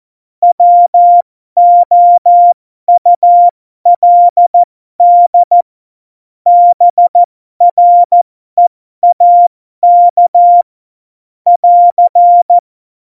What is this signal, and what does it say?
Morse code, heard as a keyed tone, about 14 wpm, spaced as for 12 wpm, 700 Hz. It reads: WOULD BREAK <AR>
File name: Morsekurs / Witze